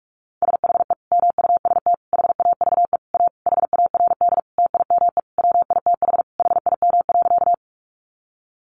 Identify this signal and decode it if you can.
Morse code, heard as a keyed tone, about 44 wpm, 700 Hz. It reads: SHE MUST HAVE A HARD TIME WITH HIM.
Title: Sentences from Top 300 Words 45wpm